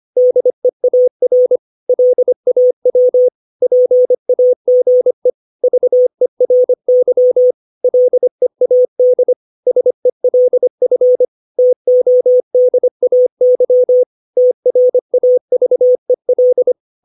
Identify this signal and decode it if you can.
Morse code, heard as a keyed tone, about 25 wpm, 500 Hz. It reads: DEAR LAW PAGE VERY LEAD SELF TODAY TRAVEL
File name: 632 words / 632 Words 25WPM